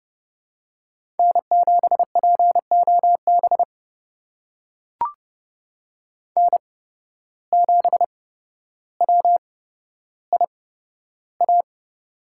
Morse code, heard as a keyed tone, about 30 wpm, spaced as for 13 wpm, 700 Hz. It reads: N7POB N7WIA
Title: USA Call Signs - 5 Characters - 8x Character Spacing - 30wpm